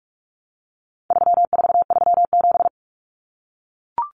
Morse code, heard as a keyed tone, about 44 wpm, 700 Hz. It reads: 3437
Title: Common Cut Numbers - 4-Digit 45wpm